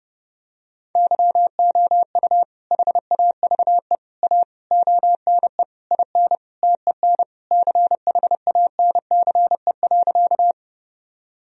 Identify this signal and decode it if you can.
Morse code, heard as a keyed tone, about 32 wpm, 700 Hz. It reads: YOU HAVE A ONE IN TEN CHANCE.
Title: Sentences from Top 1000 Words - Review Part 2 - 30wpm